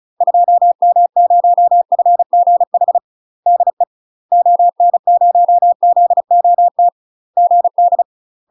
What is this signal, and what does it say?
Morse code, heard as a keyed tone, about 36 wpm, 700 Hz. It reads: 2M0FGH DE ON0ZOT GD
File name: UK MASTER QSOs TRAINING / 35WPM / UK MASTER QSOs 0002